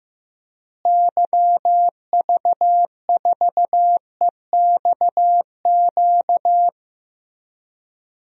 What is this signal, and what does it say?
Morse code, heard as a keyed tone, about 15 wpm, 700 Hz. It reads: YV4EXQ
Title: DXCC-100 Call Signs - 6 Characters - Repeated 2x - 15wpm